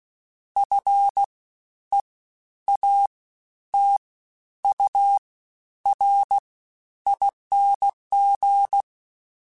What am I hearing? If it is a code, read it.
Morse code, heard as a keyed tone, about 16 wpm, 800 Hz. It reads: F E A T U R ING
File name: Special INGs-mp3s / Special INGs-mp3s / Fast ing @15